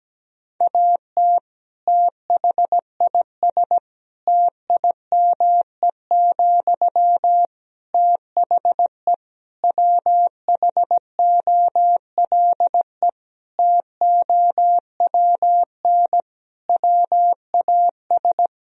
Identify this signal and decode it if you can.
Morse code, heard as a keyed tone, about 17 wpm, 700 Hz. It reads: AT THIS TIME, THE WHOLE TOWN WAS
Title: Sentences from Top 1000 Words - Review Part 2 - 17wpm